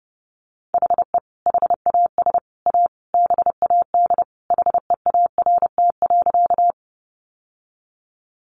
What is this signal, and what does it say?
Morse code, heard as a keyed tone, about 30 wpm, 700 Hz. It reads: HE HAS A BAD HEART.
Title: Sentences from Top 1200 Words 30wpm